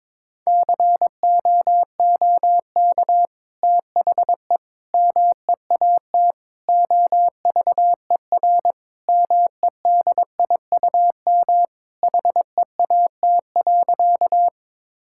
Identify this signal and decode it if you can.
Morse code, heard as a keyed tone, about 22 wpm, 700 Hz. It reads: COOK THE MEAT OVER MEDIUM HEAT.
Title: Sentences from Top 1000 Words - Review Part 6 - 22wpm